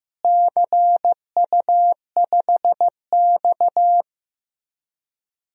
Morse code, heard as a keyed tone, about 15 wpm, 700 Hz. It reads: CU5X
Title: DXCC-100 Call Signs - 4 Characters - Repeated 2x - 15wpm